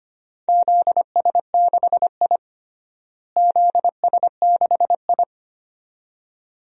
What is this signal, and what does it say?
Morse code, heard as a keyed tone, about 25 wpm, 700 Hz. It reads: ZS6I ZS6I
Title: DXCC-100 Call Signs - 4 Characters - Repeated 2x - 25wpm